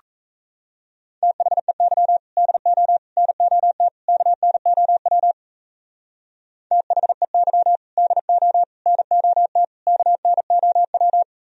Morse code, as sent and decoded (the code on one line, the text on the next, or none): - .... -.-- / -.. --- / -. --- - / -.- -. --- .-- / - .... -.-- / -.. --- / -. --- - / -.- -. --- .--
THEY DO NOT KNOW THEY DO NOT KNOW